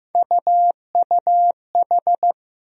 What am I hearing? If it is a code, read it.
Morse code, heard as a keyed tone, about 15 wpm, 700 Hz. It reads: UUH
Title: USA Call Sign Postfixes - 3 Letters 15wpm